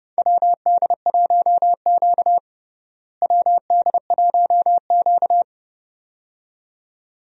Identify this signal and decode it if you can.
Morse code, heard as a keyed tone, about 30 wpm, 700 Hz. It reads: WD1Q WD1Q
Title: USA Call Signs - 4 Characters - Repeated 2x - 30wpm